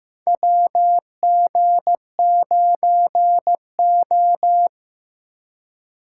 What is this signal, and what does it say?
Morse code, heard as a keyed tone, about 15 wpm, 700 Hz. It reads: WG9O
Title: USA Call Signs - 4 Characters - 2x Character Spacing - 15wpm